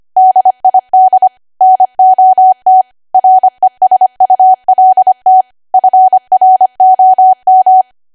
Morse code, heard as a keyed tone, about 25 wpm, 750 Hz. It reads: DID NOT RESULT FROM
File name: W1AW-Level3/140218 25WPM